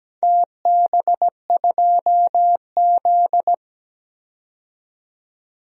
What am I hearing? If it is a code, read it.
Morse code, heard as a keyed tone, about 17 wpm, 700 Hz. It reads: TB2Z
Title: DXCC-100 Call Signs - 4 Characters - 17wpm